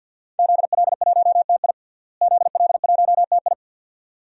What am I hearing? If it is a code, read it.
Morse code, heard as a keyed tone, about 47 wpm, 700 Hz. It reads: ZL1TI ZL1TI
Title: DXCC-100 Call Signs - 5 Characters - Repeated 2x - 50wpm